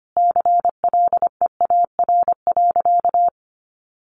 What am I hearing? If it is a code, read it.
Morse code, heard as a keyed tone, about 25 wpm, 700 Hz. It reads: CLEAR.
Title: Sentences from Top 1300 Words 25wpm